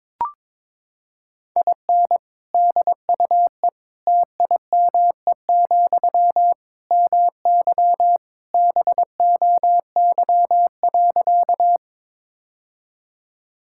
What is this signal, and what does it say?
Morse code, heard as a keyed tone, about 22 wpm, 700 Hz. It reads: IN DUE TIME, MY BOY.
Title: Sentences from Top 1000 Words - Review Part 6 - 22wpm